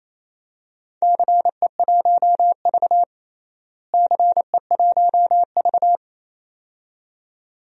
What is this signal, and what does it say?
Morse code, heard as a keyed tone, about 28 wpm, 700 Hz. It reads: CE1V CE1V
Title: DXCC-100 Call Signs - 4 Characters - Repeated 2x - 28wpm